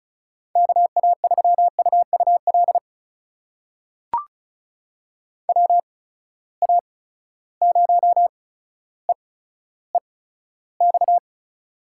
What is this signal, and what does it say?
Morse code, heard as a keyed tone, about 36 wpm, spaced as for 15 wpm, 700 Hz. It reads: KA3UUL WA0EEX
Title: USA Call Signs - 6 Characters - 8x Character Spacing - 35wpm